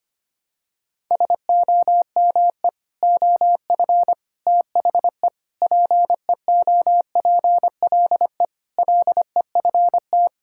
Morse code, heard as a keyed tone, about 25 wpm, 700 Hz. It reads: SOME OF THE PEOPLE LEFT
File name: practice / 25wpm / Sentences with 3-Word Phrases - 25wpm